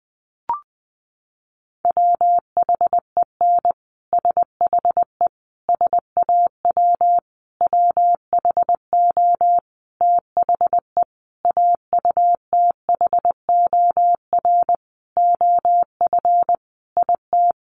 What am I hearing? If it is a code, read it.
Morse code, heard as a keyed tone, about 20 wpm, 700 Hz. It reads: WHEN SHE SAW WHO THE AUTHOR OF IT
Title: Sentences from Top 1200 Words 20wpm